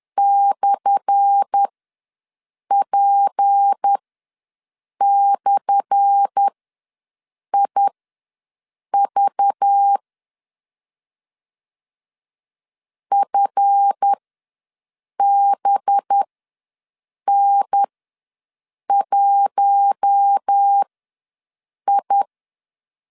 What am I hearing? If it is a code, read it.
Morse code, heard as a keyed tone, about 10 wpm, spaced as for 6 wpm, 800 Hz. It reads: /P/IV FBN1I
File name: LEZIONI / 10 SIGNS / 10A SLASH ws 10wpm 4121